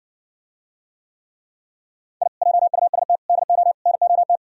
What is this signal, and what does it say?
Morse code, heard as a keyed tone, about 68 wpm, 700 Hz. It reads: I JUST DO NOT